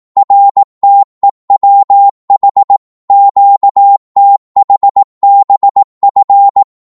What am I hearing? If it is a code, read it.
Morse code, heard as a keyed tone, about 18 wpm, 800 Hz. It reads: RTEWH QTHBF
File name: Morse / ltr-18wpm